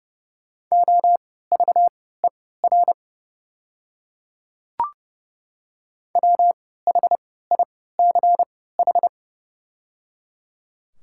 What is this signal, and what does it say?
Morse code, heard as a keyed tone, about 30 wpm, spaced as for 17 wpm, 700 Hz. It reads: OVER WHICH
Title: 3x Character Spacing - Top 100 Words - 30wpm